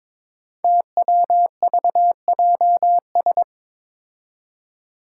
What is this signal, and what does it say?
Morse code, heard as a keyed tone, about 22 wpm, 700 Hz. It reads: TWVJS
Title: Five Challenging Letter-Number Combinations 22wpm